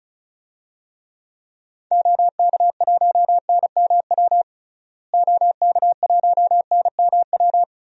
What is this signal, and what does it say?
Morse code, heard as a keyed tone, about 34 wpm, 700 Hz. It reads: OK1NMW OK1NMW
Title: DXCC-100 Call Signs - 6 Characters - Repeated 3x - 35wpm